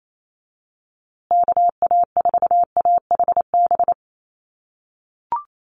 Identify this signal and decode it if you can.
Morse code, heard as a keyed tone, about 28 wpm, 700 Hz. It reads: KA4AHB
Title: USA Call Signs Repeated 2x 28wpm